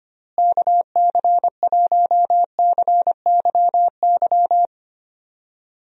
Morse code, heard as a keyed tone, about 25 wpm, 700 Hz. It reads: KC1CYY
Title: USA Call Signs - 6 Characters - Repeated 2x - 25wpm